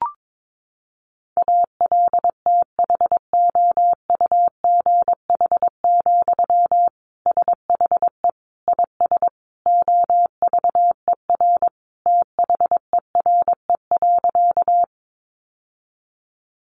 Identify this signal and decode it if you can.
Morse code, heard as a keyed tone, about 22 wpm, 700 Hz. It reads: ALTHOUGH, SHE IS OVER THERE.
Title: Sentences from Top 600 Words 22wpm